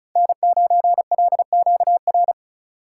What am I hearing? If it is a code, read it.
Morse code, heard as a keyed tone, about 34 wpm, 700 Hz. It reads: N9LQR
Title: USA Call Signs - 5 Characters - 4x Character Spacing - 35wpm